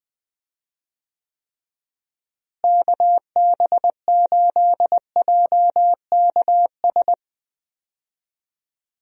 Morse code, heard as a keyed tone, about 20 wpm, 700 Hz. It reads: KB8JKS
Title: USA Call Signs - 6 Characters - Repeated 2x - 20wpm